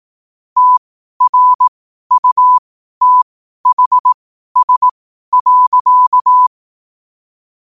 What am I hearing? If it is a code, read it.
Morse code, heard as a keyed tone, about 18 wpm, spaced as for 12 wpm, 1000 Hz. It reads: TRUTHS.
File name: qotdfiles / 12wpm